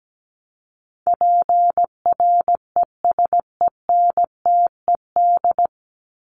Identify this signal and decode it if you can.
Morse code, heard as a keyed tone, about 17 wpm, 700 Hz. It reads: PRESENTED
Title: ED Suffix - Top 100 Words with Base Words 17wpm